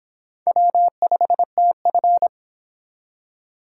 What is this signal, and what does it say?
Morse code, heard as a keyed tone, about 25 wpm, 700 Hz. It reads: W5TF